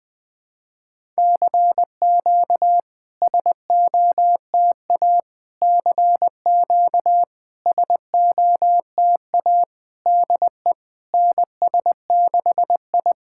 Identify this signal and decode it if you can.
Morse code, heard as a keyed tone, about 20 wpm, 700 Hz. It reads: CQ SOTA CQ SOTA DE NS6I
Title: SOTA QSO Practice 20wpm